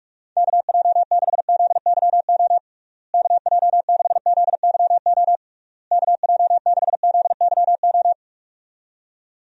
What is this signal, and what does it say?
Morse code, heard as a keyed tone, about 46 wpm, 700 Hz. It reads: KJ6ZYO KJ6ZYO KJ6ZYO
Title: USA Call Signs - 6 Characters - Repeated 3x - 45wpm